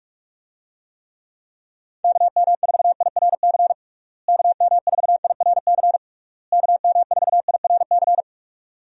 Morse code, heard as a keyed tone, about 46 wpm, 700 Hz. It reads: KM4IRC KM4IRC KM4IRC
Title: USA Call Signs - 6 Characters - Repeated 3x - 45wpm